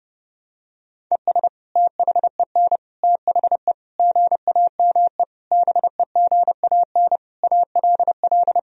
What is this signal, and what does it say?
Morse code, heard as a keyed tone, about 30 wpm, 700 Hz. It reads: ES THEN THE GAME BEGAN ALL
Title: Sentences from Top 500 Words and Abbreviations 30wpm